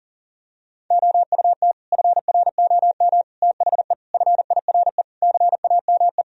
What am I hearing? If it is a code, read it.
Morse code, heard as a keyed tone, about 40 wpm, 700 Hz. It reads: OUT FROM THE FIRE CAME
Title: Sentences from Top 1000 Words - Review Part 3 - 40wpm